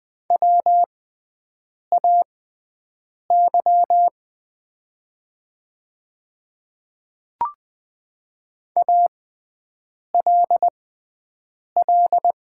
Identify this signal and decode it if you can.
Morse code, heard as a keyed tone, about 20 wpm, spaced as for 7 wpm, 700 Hz. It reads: WAY ALL
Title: 6x Character Spacing - Top 100 Words - 20wpm